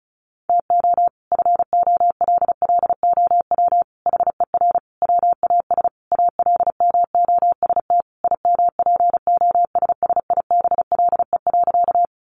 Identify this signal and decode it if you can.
Morse code, heard as a keyed tone, about 34 wpm, 700 Hz. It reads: TO FOLLOW HER WAS ALMOST IMPOSSIBLE.
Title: Sentences from Top 1100 Words 35wpm